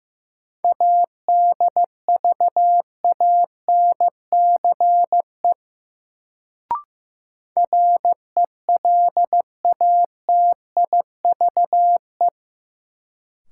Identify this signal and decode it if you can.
Morse code, heard as a keyed tone, about 15 wpm, 700 Hz. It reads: ADVANCE RELATIVE
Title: Sentences from Top 1400 Words - New Word Drill - 15wpm